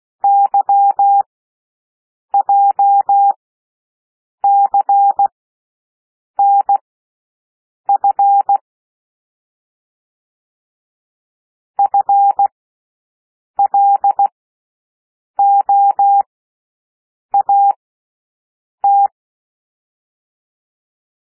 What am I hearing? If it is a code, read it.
Morse code, heard as a keyed tone, about 16 wpm, spaced as for 6 wpm, 800 Hz. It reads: YJCNF FLOAT
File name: cwmp3 / L1 SOUND LETTERS 16 wpm